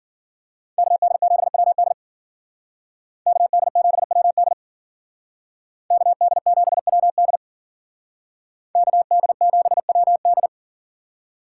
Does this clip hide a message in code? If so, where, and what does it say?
Yes: Morse code from 0.8 to 10.5 s, heard as a keyed tone, about 53 wpm, 700 Hz. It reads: KD7WD KD7WD KD7WD KD7WD